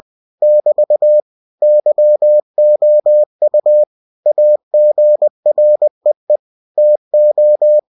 Morse code, heard as a keyed tone, about 20 wpm, 600 Hz. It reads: <BT> YOU AGREE TO
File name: cw-books / winnie-the-pooh / winnie-20wpm-page-0119